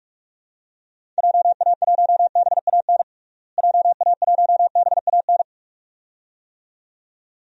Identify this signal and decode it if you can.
Morse code, heard as a keyed tone, about 46 wpm, 700 Hz. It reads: JA1BAN JA1BAN
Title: DXCC-100 Call Signs - 6 Characters - Repeated 2x - 45wpm